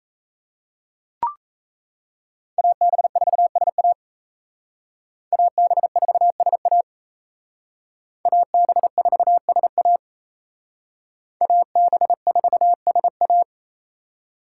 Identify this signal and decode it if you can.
Morse code, heard as a keyed tone, about 37 wpm, 700 Hz. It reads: AB4SA AB4SA AB4SA AB4SA